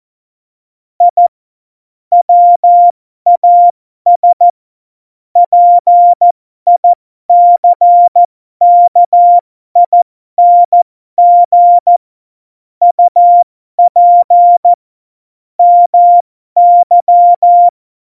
Morse code, heard as a keyed tone, about 14 wpm, spaced as for 12 wpm, 700 Hz. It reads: I WAS PICKING UP MY
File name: Morsekurs / Witze